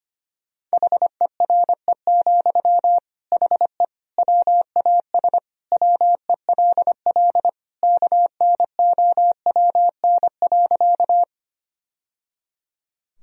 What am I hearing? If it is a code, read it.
Morse code, heard as a keyed tone, about 25 wpm, 700 Hz. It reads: HERE, HE WAS WELL KNOWN.
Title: Sentences from Top 1000 Words - Review Part 3 - 25wpm